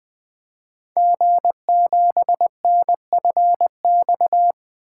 Morse code, heard as a keyed tone, about 20 wpm, 700 Hz. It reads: G7NFX
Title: Code Groups - 5 Letter:Numbers - 20wpm